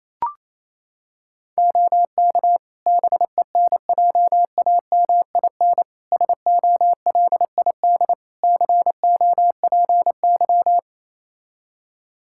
Morse code, heard as a keyed tone, about 28 wpm, 700 Hz. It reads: OK BENJAMIN SOLID COPY
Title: K1USN SST QSO Practice 28wpm